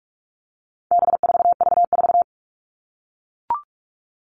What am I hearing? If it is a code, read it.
Morse code, heard as a keyed tone, about 44 wpm, 700 Hz. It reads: B4V4
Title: All Cut Numbers - 4-Digit 45wpm